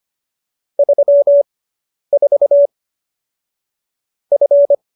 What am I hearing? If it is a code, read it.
Morse code, heard as a keyed tone, about 25 wpm, spaced as for 10 wpm, 570 Hz. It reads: 34 F